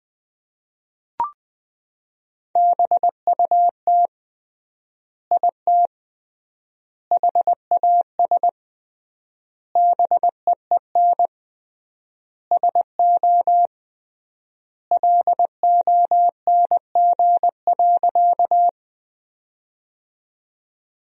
Morse code, heard as a keyed tone, about 21 wpm, 700 Hz. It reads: BUT IT HAS BEEN SO LONG.
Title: 3x Word-Spacing - Sentences from Top 100 Words - 20wpm